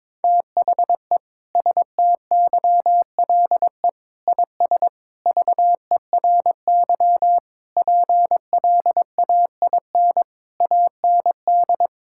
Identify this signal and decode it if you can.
Morse code, heard as a keyed tone, about 22 wpm, 700 Hz. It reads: THE STYLE IS VERY PLAIN AND
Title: Sentences from Top 1000 Words - Review Part 5 - 22wpm